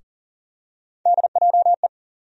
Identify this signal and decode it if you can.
Morse code, heard as a keyed tone, about 38 wpm, 700 Hz. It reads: DJE
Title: USA Call Sign Postfixes - 3 Letters 40wpm